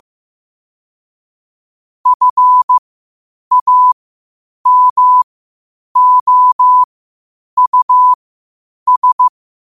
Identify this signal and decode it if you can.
Morse code, heard as a keyed tone, about 15 wpm, spaced as for 8 wpm, 1000 Hz. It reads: FAMOUS
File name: qotdfiles / 8wpm